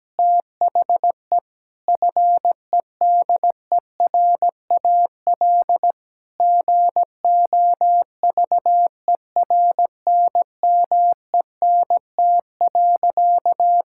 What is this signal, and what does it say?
Morse code, heard as a keyed tone, about 17 wpm, 700 Hz. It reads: THE FEDERAL GOVERNMENT.
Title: Sentences from Top 1000 Words - Review Part 5 - 17wpm